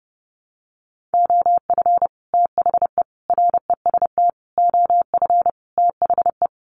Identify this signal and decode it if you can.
Morse code, heard as a keyed tone, about 28 wpm, 700 Hz. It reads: OF THE REST OF THE
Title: Sets of 5 Words 30wpm